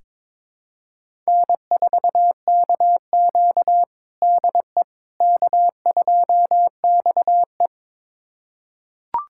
Morse code, heard as a keyed tone, about 21 wpm, 700 Hz. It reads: N4KQ DE K2XE
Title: USA Callsign DE Callsign - 4 Characters - 22wpm